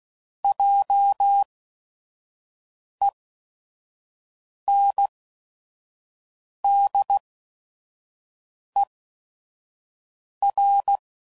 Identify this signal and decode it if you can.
Morse code, heard as a keyed tone, about 16 wpm, spaced as for 5 wpm, 800 Hz. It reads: JENDER